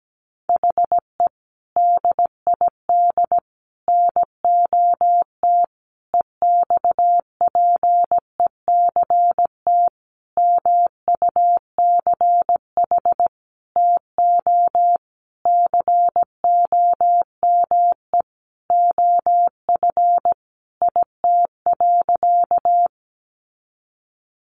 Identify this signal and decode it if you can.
Morse code, heard as a keyed tone, about 17 wpm, 700 Hz. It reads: HE DID NOT EXPECT MUCH TO COME OF IT.
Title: Sentences from Top 1100 Words 17wpm